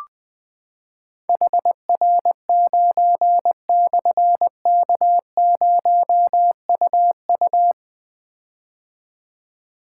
Morse code, heard as a keyed tone, about 20 wpm, 700 Hz. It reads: HR9/K0UU
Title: Difficult Call Signs 20wpm